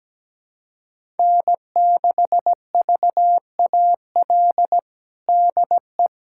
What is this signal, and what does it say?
Morse code, heard as a keyed tone, about 17 wpm, 700 Hz. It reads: N6VAL DE
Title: USA Callsign DE Callsign - 5 Characters - 17wpm